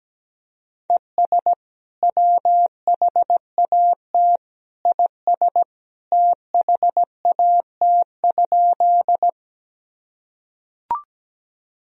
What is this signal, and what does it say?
Morse code, heard as a keyed tone, about 17 wpm, 700 Hz. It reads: ES WHAT IS THAT?
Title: Sentences from Top 100 Words and Abbreviations 17wpm